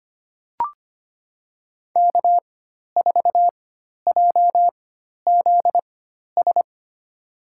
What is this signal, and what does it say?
Morse code, heard as a keyed tone, about 25 wpm, spaced as for 12 wpm, 700 Hz. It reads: K4JZS
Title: USA Call Signs - 5 Characters - 4x Character Spacing - 25wpm